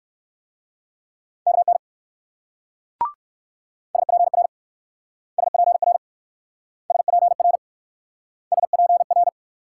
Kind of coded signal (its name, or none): Morse code